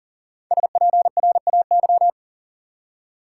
Morse code, heard as a keyed tone, about 40 wpm, 700 Hz. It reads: SPRAY
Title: Five Letter Words - 40wpm